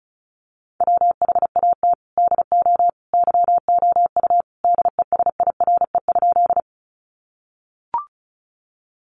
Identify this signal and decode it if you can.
Morse code, heard as a keyed tone, about 34 wpm, 700 Hz. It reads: WHAT DO YOU DESIRE?
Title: Sentences from Top 1100 Words 35wpm